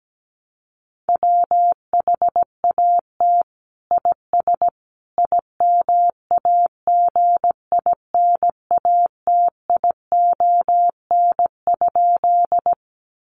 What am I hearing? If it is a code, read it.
Morse code, heard as a keyed tone, about 17 wpm, 700 Hz. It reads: WHAT IS IMAGINATION?
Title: Sentences from Top 1300 Words 17wpm